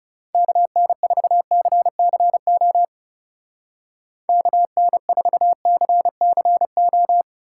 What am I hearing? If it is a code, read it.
Morse code, heard as a keyed tone, about 37 wpm, 700 Hz. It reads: KN4CCO KN4CCO
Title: Speed Racing - USA Call Signs 30wpm